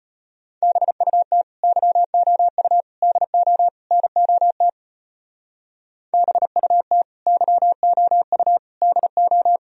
Morse code, heard as a keyed tone, about 31 wpm, 700 Hz. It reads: BUT YOU DO NOT BUT YOU DO
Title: Speed-Racing - Sets of 4 Words 25wpm